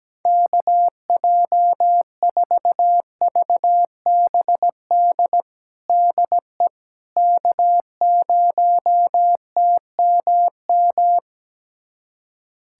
Morse code, heard as a keyed tone, about 17 wpm, 700 Hz. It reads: KJ4VBD DE K0TMM
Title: USA Callsign DE Callsign 17wpm